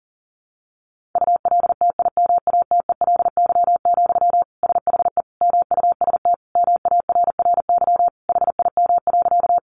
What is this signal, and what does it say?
Morse code, heard as a keyed tone, about 37 wpm, 700 Hz. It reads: ULTIMATELY, SHE MUST MARRY HIM.
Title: Sentences from Top 1200 Words 40wpm